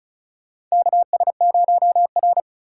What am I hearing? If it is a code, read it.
Morse code, heard as a keyed tone, about 34 wpm, 700 Hz. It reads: KS0R
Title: USA Call Signs - 4 Characters - 6x Character Spacing - 35wpm